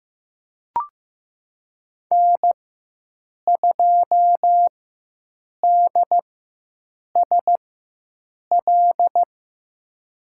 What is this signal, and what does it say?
Morse code, heard as a keyed tone, about 15 wpm, spaced as for 7 wpm, 700 Hz. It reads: N2DSL
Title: USA Call Signs - 4x Character Spacing - 15wpm